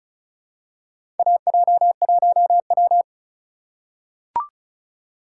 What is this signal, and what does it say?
Morse code, heard as a keyed tone, about 36 wpm, 700 Hz. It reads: AJ1W
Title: USA Call Signs - 4 Characters - 3x Character Spacing - 35wpm